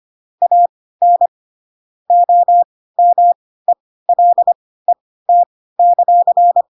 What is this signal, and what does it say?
Morse code, heard as a keyed tone, about 25 wpm, spaced as for 16 wpm, 700 Hz. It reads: AN OMELET;
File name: Pinocchio / Pinocchio ch5 10 25 ik1hge